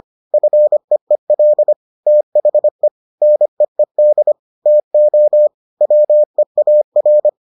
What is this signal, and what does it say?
Morse code, heard as a keyed tone, about 25 wpm, 600 Hz. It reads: FEEL THE NEED TO WEAR